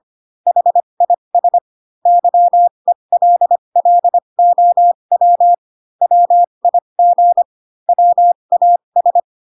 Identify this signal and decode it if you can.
Morse code, heard as a keyed tone, about 25 wpm, spaced as for 22 wpm, 700 Hz. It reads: HIS YELLOW WIG WAS
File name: Pinocchio / Pinocchio ch3 18 25 ik1hge